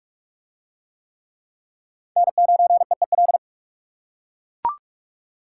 Morse code, heard as a keyed tone, about 43 wpm, 700 Hz. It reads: N9EEL